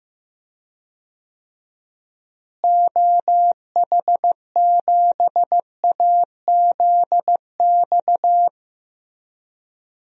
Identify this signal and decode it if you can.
Morse code, heard as a keyed tone, about 15 wpm, 700 Hz. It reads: OH7AZX
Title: DXCC-100 Call Signs - 6 Characters - Repeated 2x - 15wpm